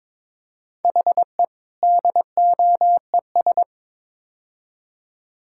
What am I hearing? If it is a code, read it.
Morse code, heard as a keyed tone, about 22 wpm, 700 Hz. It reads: HE DOES
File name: practice / 22wpm / Build A Phrase - 5-Word Phrases - 22wpm